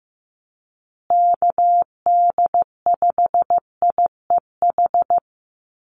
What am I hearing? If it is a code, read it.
Morse code, heard as a keyed tone, about 15 wpm, 700 Hz. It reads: KD5IEH
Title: USA Call Signs Repeated 2x 15wpm